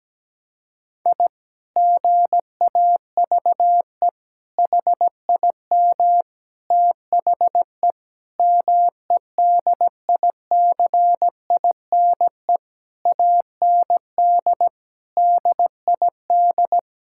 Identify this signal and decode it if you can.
Morse code, heard as a keyed tone, about 17 wpm, 700 Hz. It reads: I GAVE HIM THE MEDICINE AND DID
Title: Sentences from Top 1000 Words - Review Part 6 - 17wpm